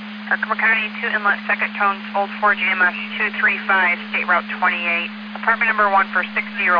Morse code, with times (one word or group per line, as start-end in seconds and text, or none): none